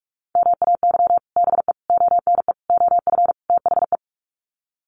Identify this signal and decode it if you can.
Morse code, heard as a keyed tone, about 44 wpm, 700 Hz. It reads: MAY BE ONE OF THE
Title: Sets of 5 Words Encore2 45wpm